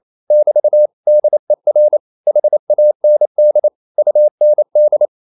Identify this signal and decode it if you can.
Morse code, heard as a keyed tone, about 28 wpm, 600 Hz. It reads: <BT> DER HAND UND
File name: marienkind9 28wpm